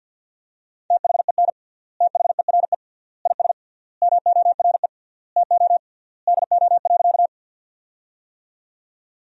Morse code, heard as a keyed tone, about 50 wpm, 700 Hz. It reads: THEN THERE IS MORE TO DO.
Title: 3x Word-Spacing - Sentences from Top 100 Words - 50wpm